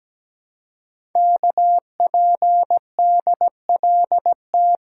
Morse code, heard as a keyed tone, about 17 wpm, 700 Hz. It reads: KPDLT
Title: Code Groups - 5 Letters - 17wpm